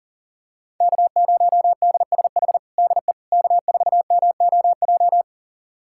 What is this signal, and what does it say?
Morse code, heard as a keyed tone, about 40 wpm, 700 Hz. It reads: K0DSH DE K4MOJ
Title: USA Callsign DE Callsign - 5 Characters - 40wpm